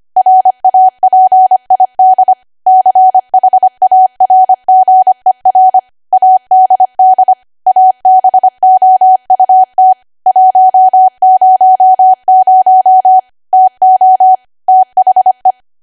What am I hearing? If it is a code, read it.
Morse code, heard as a keyed tone, about 25 wpm, 750 Hz. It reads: RAPID CHARGER ADD ABOUT 100 TO THE
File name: Archive / 25 WPM / 25WPM